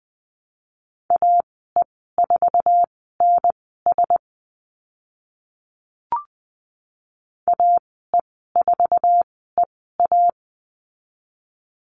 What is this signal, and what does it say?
Morse code, heard as a keyed tone, about 20 wpm, spaced as for 14 wpm, 700 Hz. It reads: AE4NS AE4EA
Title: 062 Lesson 11g - Letter h - Call Signs 20wpm